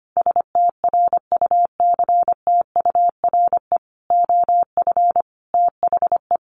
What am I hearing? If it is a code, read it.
Morse code, heard as a keyed tone, about 25 wpm, 700 Hz. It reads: STRUCTURE OF THE
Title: Sentences from Top 1500 Words 25wpm